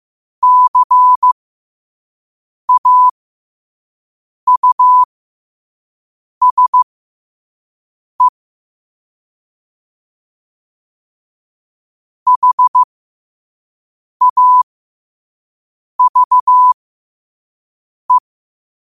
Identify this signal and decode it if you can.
Morse code, heard as a keyed tone, about 15 wpm, spaced as for 5 wpm, 1000 Hz. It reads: CAUSE HAVE
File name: qotdfiles / 5wpm